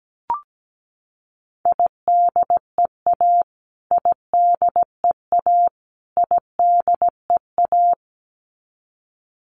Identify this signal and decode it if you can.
Morse code, heard as a keyed tone, about 17 wpm, 700 Hz. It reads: IDEA IDEA IDEA
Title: Top 500 Words Repeated 17wpm